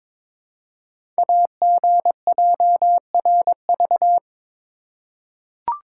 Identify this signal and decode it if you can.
Morse code, heard as a keyed tone, about 22 wpm, 700 Hz. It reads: AGJRV
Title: Code Groups - 5 Letter:Numbers - 22wpm